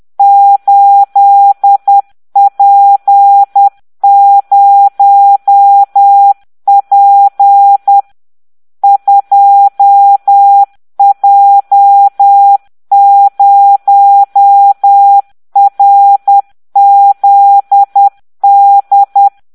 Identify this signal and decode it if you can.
Morse code, heard as a keyed tone, about 10 wpm, 800 Hz. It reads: 8P0P 2J0RZD